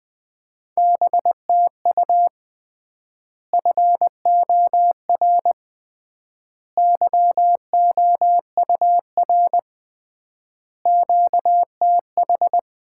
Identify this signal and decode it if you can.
Morse code, heard as a keyed tone, about 20 wpm, 700 Hz. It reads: BTU FOR YOUR QTH
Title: Common QSO Phrases - 3x Word-Spacing - 20wpm